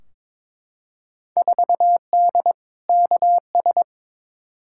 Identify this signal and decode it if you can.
Morse code, heard as a keyed tone, about 22 wpm, 700 Hz. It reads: 4D KS
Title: Field Day Exchanges Encore - 22wpm